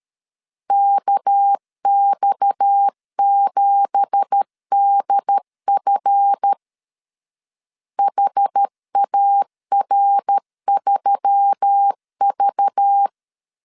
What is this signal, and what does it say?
Morse code, heard as a keyed tone, about 13 wpm, 800 Hz. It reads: KX7DF HAR3V
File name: CW / DET / 1 9 1500C S C 12wpm 3811